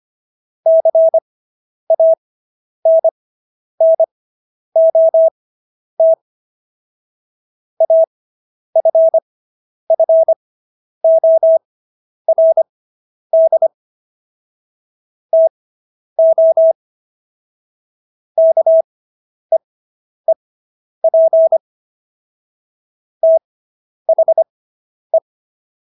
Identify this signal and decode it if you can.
Morse code, heard as a keyed tone, about 25 wpm, spaced as for 10 wpm, 650 Hz. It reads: CANNOT AFFORD TO KEEP THE